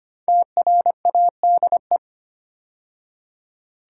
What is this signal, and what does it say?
Morse code, heard as a keyed tone, about 25 wpm, 700 Hz. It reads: TRADE
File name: 5 Letter Words - First 3 Letters Given - 25wpm